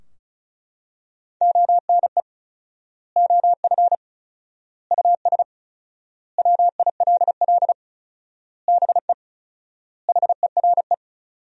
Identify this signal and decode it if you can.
Morse code, heard as a keyed tone, about 36 wpm, 700 Hz. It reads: ONE OF US WILL BE HERE
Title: 4x Word-Spacing - Sentences from Top 300 Words - 35wpm